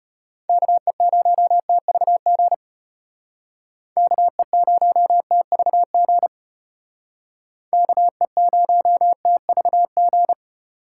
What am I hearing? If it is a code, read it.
Morse code, heard as a keyed tone, about 34 wpm, 700 Hz. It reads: KE0TVG KE0TVG KE0TVG